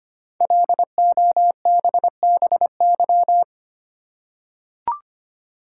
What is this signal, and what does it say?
Morse code, heard as a keyed tone, about 25 wpm, 700 Hz. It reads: LOBBY
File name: 5 Letter Words - First 3 Letters Given - 25wpm